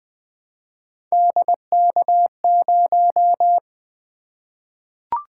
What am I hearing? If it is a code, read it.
Morse code, heard as a keyed tone, about 20 wpm, 700 Hz. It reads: DK0
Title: Three Letter-Number Combinations - 20wpm